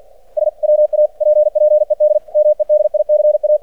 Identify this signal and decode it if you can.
Morse code, heard as a keyed tone, about 47 wpm, 600 Hz. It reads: SWAPPED MEDICA
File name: archived / CRSnet